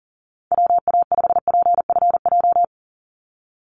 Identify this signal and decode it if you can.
Morse code, heard as a keyed tone, about 40 wpm, 700 Hz. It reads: WA5PFJ
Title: Call Signs Repeated 2x - 40wpm